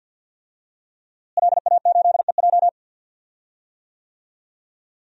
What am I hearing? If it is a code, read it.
Morse code, heard as a keyed tone, about 52 wpm, 700 Hz. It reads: LA8EJ
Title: DXCC-100 Call Signs - 5 Characters - Repeated 2x - 50wpm